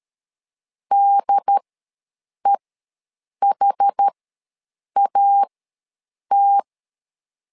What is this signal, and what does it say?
Morse code, heard as a keyed tone, about 13 wpm, spaced as for 7 wpm, 800 Hz. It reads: DEHAT
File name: LEZIONI / 2 UHDN / 2 UHD ws 12wpm 1722